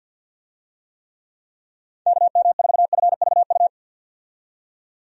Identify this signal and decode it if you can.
Morse code, heard as a keyed tone, about 52 wpm, 700 Hz. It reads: KM4FVU
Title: USA Call Signs - 6 Characters - 50wpm